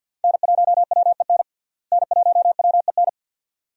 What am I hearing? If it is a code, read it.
Morse code, heard as a keyed tone, about 50 wpm, 700 Hz. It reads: N1WEN N1WEN
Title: USA Call Signs - 5 Characters - Repeated 2x - 50wpm